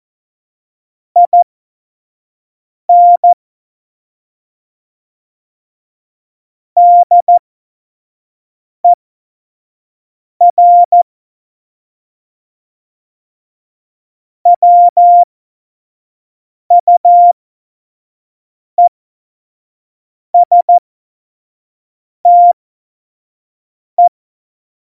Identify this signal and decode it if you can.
Morse code, heard as a keyed tone, about 14 wpm, spaced as for 5 wpm, 700 Hz. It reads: IN DER WUESTE